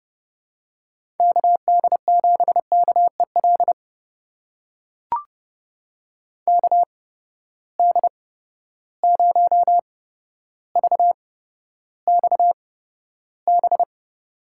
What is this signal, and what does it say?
Morse code, heard as a keyed tone, about 31 wpm, spaced as for 13 wpm, 700 Hz. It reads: KD7KEL KD0VXB
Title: USA Call Signs - 6 Characters - 8x Character Spacing - 30wpm